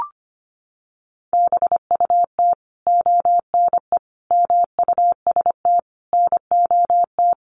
Morse code, heard as a keyed tone, about 25 wpm, 700 Hz. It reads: BUT ONE MUST NOT
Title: Sentences from Top 1500 Words 25wpm